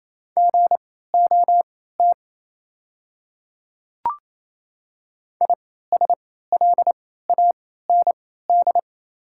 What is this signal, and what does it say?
Morse code, heard as a keyed tone, about 28 wpm, spaced as for 16 wpm, 690 Hz. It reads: GOT ISLAND
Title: 3x Character Spacing - Top 500 Words - 28wpm